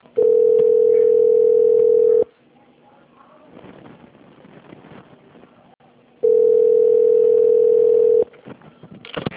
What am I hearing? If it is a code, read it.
Ringback tone (North America), 440 + 480 Hz.
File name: out- - - - - .37485